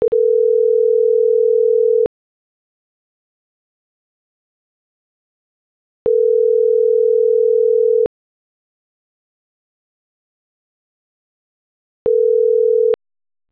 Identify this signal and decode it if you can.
Ringback tone (North America), 440 + 480 Hz.